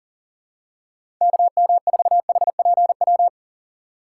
Morse code, heard as a keyed tone, about 37 wpm, 700 Hz. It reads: KM4HPW
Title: USA Call Signs - 6 Characters - 4x Character Spacing - 40wpm